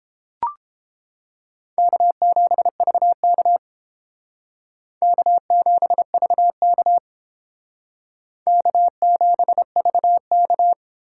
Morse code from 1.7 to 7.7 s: K7VK K7VK

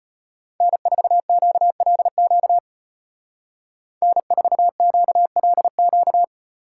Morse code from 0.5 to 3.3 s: N4QLQ

